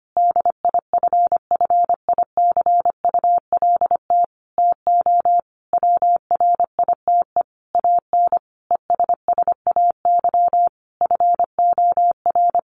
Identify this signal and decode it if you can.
Morse code, heard as a keyed tone, about 25 wpm, 700 Hz. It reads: DIFFICULT TO WRITE AN ESSAY FOR
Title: Sentences from Top 1500 Words 25wpm